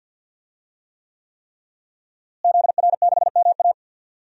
Morse code, heard as a keyed tone, about 50 wpm, 700 Hz. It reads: ZR6MA
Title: DXCC-100 Call Signs - 5 Characters - Repeated 3x - 50wpm